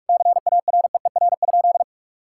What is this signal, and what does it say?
Morse code, heard as a keyed tone, about 41 wpm, 700 Hz. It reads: CAREER?
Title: Sentences from Top 1000 Words - Review Part 6 - 45wpm